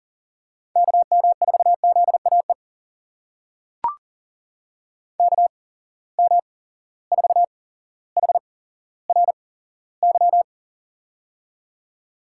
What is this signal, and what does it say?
Morse code, heard as a keyed tone, about 40 wpm, spaced as for 17 wpm, 700 Hz. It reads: KM4ZAE KM4HRY
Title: USA Call Signs - 6 Characters - 8x Character Spacing - 40wpm